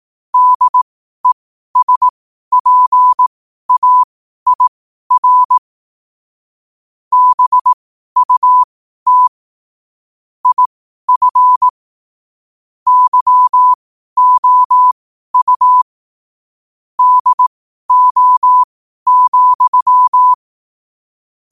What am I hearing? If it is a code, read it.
Morse code, heard as a keyed tone, about 18 wpm, spaced as for 12 wpm, 1000 Hz. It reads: DESPAIR BUT IF YOU DO,
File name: qotdfiles / 12wpm